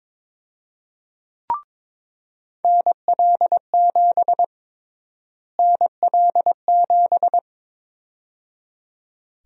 Morse code, heard as a keyed tone, about 22 wpm, 700 Hz. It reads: NL7 NL7
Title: USA Call Sign Prefixes - Repeated 2x - 22wpm